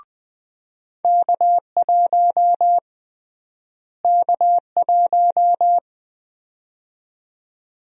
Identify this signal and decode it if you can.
Morse code, heard as a keyed tone, about 20 wpm, 700 Hz. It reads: K1 K1
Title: USA Call Sign Prefixes - Repeated 2x - 20wpm